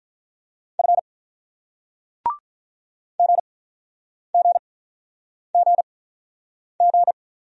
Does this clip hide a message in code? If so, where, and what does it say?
Yes: Morse code from 0.8 to 7.1 s, heard as a keyed tone, about 48 wpm, spaced as for 9 wpm, 700 Hz. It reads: F GGGG